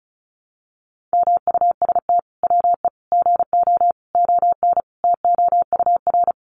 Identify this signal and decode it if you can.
Morse code, heard as a keyed tone, about 34 wpm, 700 Hz. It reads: MUST WE GO ON TOUR
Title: Sentences from Top 1400 Words 35wpm